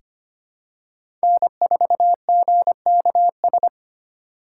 Morse code, heard as a keyed tone, about 25 wpm, 700 Hz. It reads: N4GKS
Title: USA Call Signs - 5 Characters - 3x Character Spacing - 25wpm